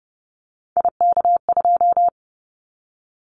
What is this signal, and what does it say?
Morse code, heard as a keyed tone, about 30 wpm, 700 Hz. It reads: IK2
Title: Callsign Prefixes for Top 25 DXCC - 30wpm